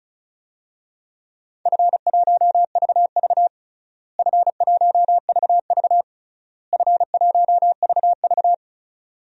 Morse code, heard as a keyed tone, about 34 wpm, 700 Hz. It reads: F1VV F1VV F1VV
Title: DXCC-100 Call Signs - 4 Characters - Repeated 3x - 35wpm